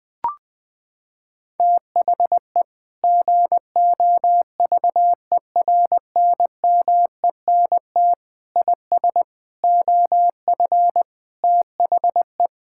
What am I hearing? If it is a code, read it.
Morse code, heard as a keyed tone, about 20 wpm, 700 Hz. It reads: THE GOVERNMENT IS OF THE
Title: Sentences with 9-Word Phrases - 20wpm